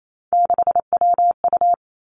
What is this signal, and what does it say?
Morse code, heard as a keyed tone, about 28 wpm, 700 Hz. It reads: 6WU
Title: Three Challenging Letter-Number Combinations 28wpm